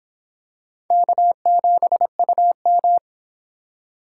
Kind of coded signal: Morse code

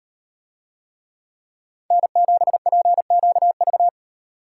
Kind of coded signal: Morse code

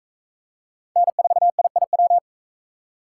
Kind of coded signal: Morse code

